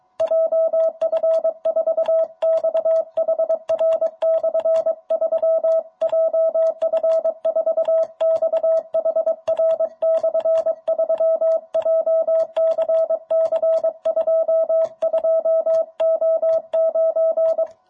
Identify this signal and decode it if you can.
Morse code, heard as a keyed tone, about 22 wpm, 660 Hz. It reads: JF4XHR/3 JF4XHR/3 JCC 22O9
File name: jh8cbh / 2021cw